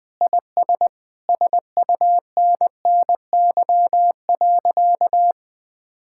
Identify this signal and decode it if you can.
Morse code, heard as a keyed tone, about 20 wpm, 700 Hz. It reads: IS SUNNY.
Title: Sentences with 2-Word Phrases - Spoken Beginning - 20wpm